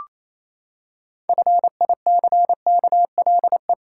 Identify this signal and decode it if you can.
Morse code, heard as a keyed tone, about 28 wpm, 700 Hz. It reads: FICKLE
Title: Pangram Challenge v2 - 28wpm